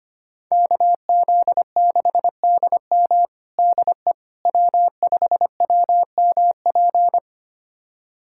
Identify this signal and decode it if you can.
Morse code, heard as a keyed tone, about 25 wpm, 700 Hz. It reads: KZ6DM DE W5WMP
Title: USA Callsign DE Callsign - 5 Characters - 25wpm